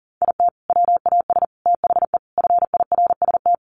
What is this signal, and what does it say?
Morse code, heard as a keyed tone, about 40 wpm, 700 Hz. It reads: IT WAS THE FIRST
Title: Sets of 4 Words 40wpm